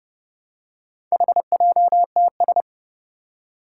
Morse code, heard as a keyed tone, about 30 wpm, 700 Hz. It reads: HJTS
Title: Four Challenging Letter-Number Combinations 30wpm